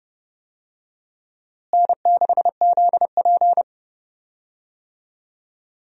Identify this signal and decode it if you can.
Morse code, heard as a keyed tone, about 30 wpm, 700 Hz. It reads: N6ZP
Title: USA Call Signs - 4 Characters - Repeated 3x - 30wpm